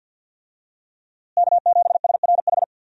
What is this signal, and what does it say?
Morse code, heard as a keyed tone, about 52 wpm, 700 Hz. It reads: K7SRH
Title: USA Call Signs - 5 Characters - Repeated 3x - 50wpm